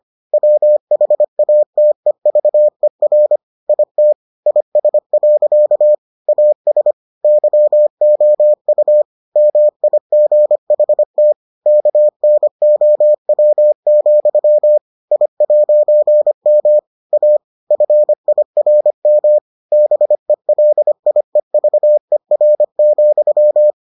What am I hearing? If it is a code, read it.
Morse code, heard as a keyed tone, about 25 wpm, 600 Hz. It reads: WHATEVER IT IS. AS YOU MIGHT KNOW, I'M A FIRM BELIEVER,